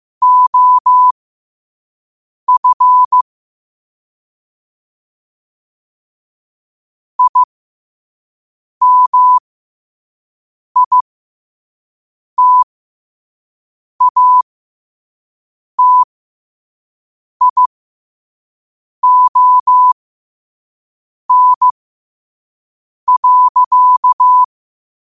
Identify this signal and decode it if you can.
Morse code, heard as a keyed tone, about 15 wpm, spaced as for 5 wpm, 1000 Hz. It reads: OF IMITATION.
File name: qotdfiles / 5wpm